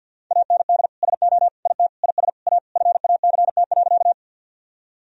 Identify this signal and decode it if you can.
Morse code, heard as a keyed tone, about 52 wpm, 700 Hz. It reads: AND SO IT IS A FACT.
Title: Sentences from Top 1000 Words - Review Part 3 - 50wpm